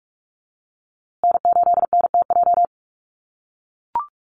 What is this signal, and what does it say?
Morse code, heard as a keyed tone, about 44 wpm, 700 Hz. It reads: N8NTJ
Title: USA Call Signs Repeated 3x 45wpm